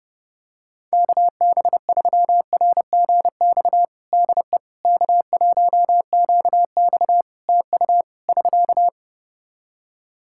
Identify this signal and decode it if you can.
Morse code, heard as a keyed tone, about 30 wpm, 700 Hz. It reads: KB3RGX DE K1QX TU <SK>
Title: Practice QSO - Exchange 4 - Ending - 30wpm